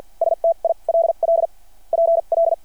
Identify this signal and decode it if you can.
Morse code, heard as a keyed tone, about 51 wpm, spaced as for 39 wpm, 660 Hz. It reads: STILL PL
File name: archived / CRSnet